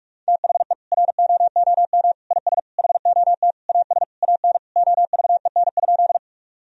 Morse code, heard as a keyed tone, about 46 wpm, 700 Hz. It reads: THE ROOM IS HOT AS AN OVEN?
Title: Sentences from Top 1000 Words - Review Part 4 - 45wpm